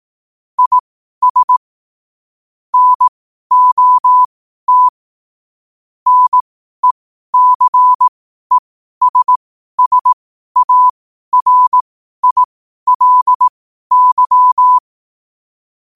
Morse code, heard as a keyed tone, about 18 wpm, spaced as for 12 wpm, 1000 Hz. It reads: IS NOT NECESSARILY
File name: qotdfiles / 12wpm